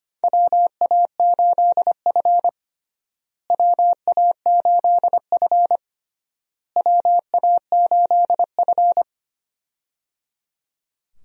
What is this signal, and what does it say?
Morse code, heard as a keyed tone, about 25 wpm, 700 Hz. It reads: WA8F WA8F WA8F
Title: USA Call Signs - 4 Characters - Repeated 3x - 25wpm